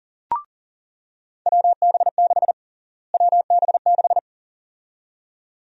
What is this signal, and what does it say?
Morse code, heard as a keyed tone, about 40 wpm, 700 Hz. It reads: WB6 WB6
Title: USA Call Sign Prefixes - Repeated 2x - 40wpm